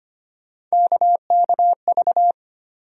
Morse code, heard as a keyed tone, about 25 wpm, 700 Hz. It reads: KKV
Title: USA Call Sign Postfixes - 3 Letters 25wpm